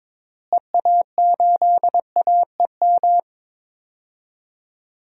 Morse code, heard as a keyed tone, about 21 wpm, 700 Hz. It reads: EA8AEM
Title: DXCC-100 Call Signs - 6 Characters - Repeated 3x - 22wpm